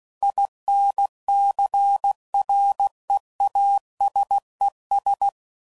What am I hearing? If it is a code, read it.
Morse code, heard as a keyed tone, about 16 wpm, 800 Hz. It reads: INCREASES
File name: CW Code Groups @15WPM / Technical Subjects / XL